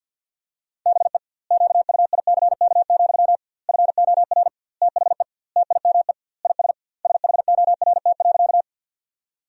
Morse code, heard as a keyed tone, about 51 wpm, 700 Hz. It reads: BE QUICK, FOR THE TIME IS SHORT.
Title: 2x Word-Spacing - Sentences from Top 500 Words - 50wpm